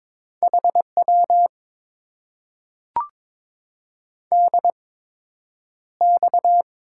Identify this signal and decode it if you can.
Morse code, heard as a keyed tone, about 22 wpm, spaced as for 10 wpm, 700 Hz. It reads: HW DX
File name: Top 90 QSO Elements - 8x Character Spacing - 22wpm